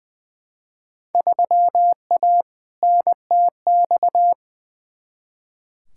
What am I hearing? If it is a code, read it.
Morse code, heard as a keyed tone, about 21 wpm, 700 Hz. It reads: 3A NTX
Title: Field Day Exchanges - 20wpm